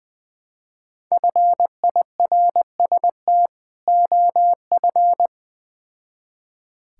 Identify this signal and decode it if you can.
Morse code, heard as a keyed tone, about 21 wpm, 700 Hz. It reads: FIRST OF
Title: Build A Phrase - 3-Word Phrases - 20wpm